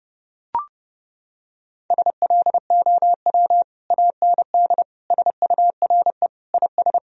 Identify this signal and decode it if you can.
Morse code, heard as a keyed tone, about 32 wpm, 700 Hz. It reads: SLOW AND SURE IS
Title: Sentences from Top 1000 Words - Review Part 2 - 30wpm